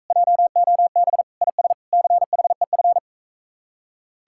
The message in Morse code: JOB IS CHEF

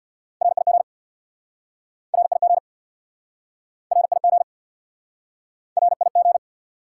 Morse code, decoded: RIG RIG RIG RIG

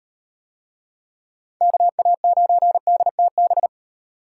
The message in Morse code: KA9DTB

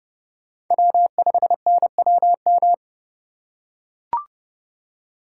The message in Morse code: W5NWM

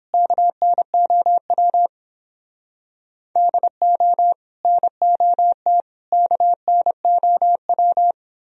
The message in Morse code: KNOW DO NOT KNOW

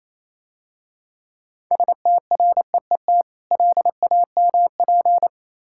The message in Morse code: STREET LAMP

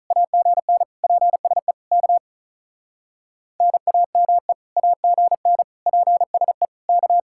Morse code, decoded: AGN PSE K NAME AGN PSE K

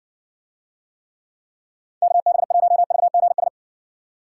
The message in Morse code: KB1VGS

